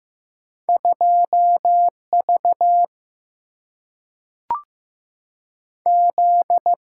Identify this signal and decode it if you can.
Morse code, heard as a keyed tone, about 15 wpm, 700 Hz. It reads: 2 V Z
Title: Single-Letter Single-Number 15wpm